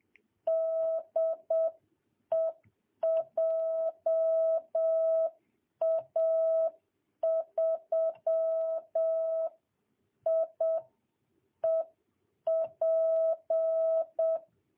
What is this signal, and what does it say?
Morse code, heard as a keyed tone, about 7 wpm, 660 Hz. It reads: DE JA3IEP
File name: jh8cbh / 2022cw